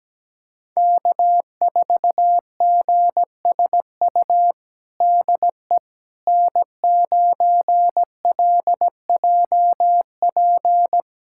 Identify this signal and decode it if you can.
Morse code, heard as a keyed tone, about 17 wpm, 700 Hz. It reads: K4GSU DE N9LJP
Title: USA Callsign DE Callsign 17wpm